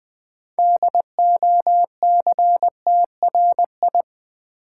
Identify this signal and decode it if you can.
Morse code, heard as a keyed tone, about 20 wpm, 700 Hz. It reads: DOCTRI
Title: Build A Word - 3 Letters At A Time - 8-Letter Words - 20wpm